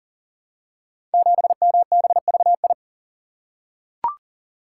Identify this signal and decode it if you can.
Morse code, heard as a keyed tone, about 39 wpm, 700 Hz. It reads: 7MBVI
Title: Code Groups - 5 Letter:Numbers - 40wpm